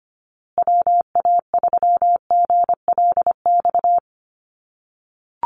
Morse code, heard as a keyed tone, about 25 wpm, 700 Hz. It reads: WA3GLX
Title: USA Call Signs Repeated 2x 25wpm